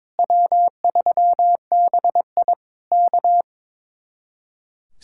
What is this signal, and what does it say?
Morse code, heard as a keyed tone, about 22 wpm, 700 Hz. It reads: W3BI K
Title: Speed-Racing Common QSO Phrases - 22wpm